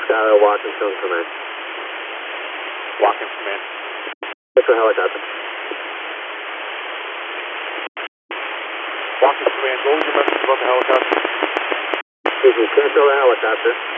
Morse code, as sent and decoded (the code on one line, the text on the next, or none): none